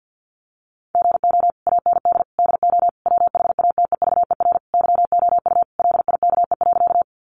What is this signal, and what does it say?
Morse code, heard as a keyed tone, about 50 wpm, 700 Hz. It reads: GO AND DO WHATEVER YOU LIKE.
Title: Sentences from Top 1100 Words 50wpm